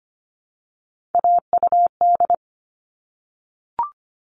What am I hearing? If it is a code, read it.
Morse code, heard as a keyed tone, about 25 wpm, 700 Hz. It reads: AUD
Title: All Cut Numbers - 3-Digit 25wpm